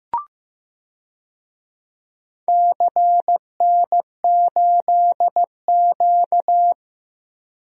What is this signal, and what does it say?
Morse code, heard as a keyed tone, about 15 wpm, 700 Hz. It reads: CN8Q
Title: DXCC-100 Call Signs - 4 Characters - Repeated 2x - 15wpm